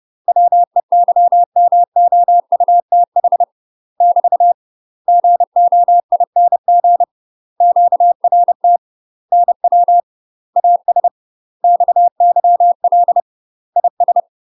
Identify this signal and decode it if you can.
Morse code, heard as a keyed tone, about 30 wpm, 700 Hz. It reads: WEYMOUTH <BT> GOING QRT NW AS XYL IS
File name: UK MASTER QSOs TRAINING / 30WPM / UK MASTER QSOs 0030